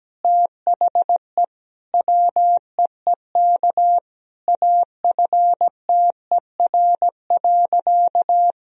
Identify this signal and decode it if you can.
Morse code, heard as a keyed tone, about 17 wpm, 700 Hz. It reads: THE WEEK AFTER.
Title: Sentences from Top 500 Words and Abbreviations 17wpm